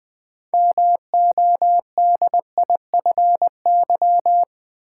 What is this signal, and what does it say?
Morse code, heard as a keyed tone, about 20 wpm, 700 Hz. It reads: MODIFY
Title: Build A Word - 3 Letters At A Time - 6-Letter Words - 20wpm